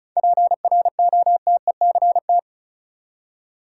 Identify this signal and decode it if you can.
Morse code, heard as a keyed tone, about 34 wpm, 700 Hz. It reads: PROTECT
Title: Seven Letter Words - 35wpm